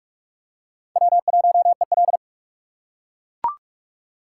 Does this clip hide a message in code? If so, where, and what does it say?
Yes: Morse code from 1.0 to 2.2 s, heard as a keyed tone, about 40 wpm, 700 Hz. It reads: W1EL